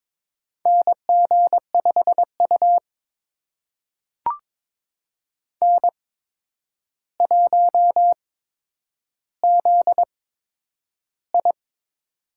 Morse code, heard as a keyed tone, about 22 wpm, spaced as for 9 wpm, 700 Hz. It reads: NG5U N1ZI
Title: USA Call Signs - 4 Characters - 8x Character Spacing - 22wpm